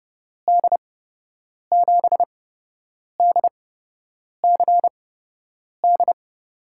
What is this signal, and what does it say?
Morse code, heard as a keyed tone, about 30 wpm, spaced as for 8 wpm, 700 Hz. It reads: D7DCD
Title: USA Call Signs - 6 Characters - 8x Character Spacing - 30wpm